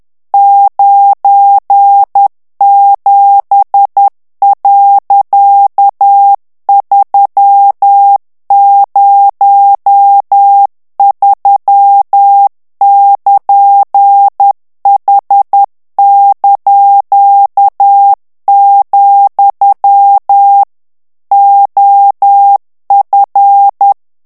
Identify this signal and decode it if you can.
Morse code, heard as a keyed tone, about 10 wpm, 800 Hz. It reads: 97.303<KN>H), OF
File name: ANGLAIS texte clair 3 vitesse 10 mots minute